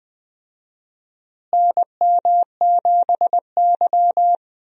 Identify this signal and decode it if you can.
Morse code, heard as a keyed tone, about 20 wpm, 700 Hz. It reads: NM7Y
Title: USA Call Signs - 4 Characters - Repeated 2x - 20wpm